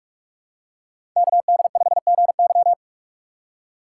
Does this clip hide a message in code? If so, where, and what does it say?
Yes: Morse code from 1.2 to 2.7 s, heard as a keyed tone, about 45 wpm, 700 Hz. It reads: KD5GY